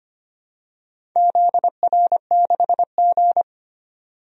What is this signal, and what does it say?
Morse code, heard as a keyed tone, about 25 wpm, 700 Hz. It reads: ZR6G
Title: DXCC-100 Call Signs - 4 Characters - 25wpm